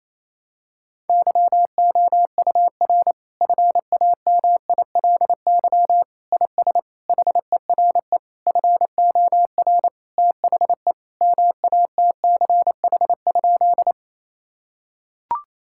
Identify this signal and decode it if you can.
Morse code, heard as a keyed tone, about 28 wpm, 700 Hz. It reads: YOUR FAMILY IS HERE FOR THE MATCH?
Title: Sentences from Top 1600 Words 28wpm